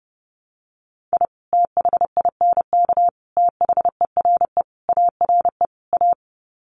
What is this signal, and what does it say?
Morse code, heard as a keyed tone, about 30 wpm, 700 Hz. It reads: I THINK THERE ARE A
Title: Sets of 5 Words Encore2 30wpm